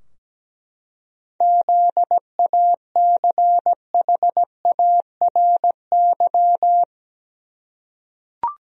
Morse code, heard as a keyed tone, about 17 wpm, 700 Hz. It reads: ZACHARY
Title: USA Common Names - 17wpm